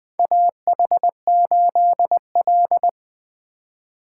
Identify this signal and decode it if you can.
Morse code, heard as a keyed tone, about 20 wpm, 700 Hz. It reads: AH8L
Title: USA Call Signs - 4 Characters - 2x Character Spacing - 20wpm